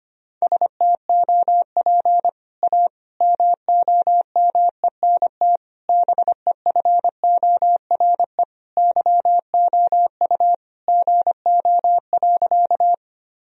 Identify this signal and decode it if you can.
Morse code, heard as a keyed tone, about 25 wpm, 700 Hz. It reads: STOP A MOMENT BEFORE YOU GO.
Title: Sentences from Top 1000 Words - Review Part 1 - 25wpm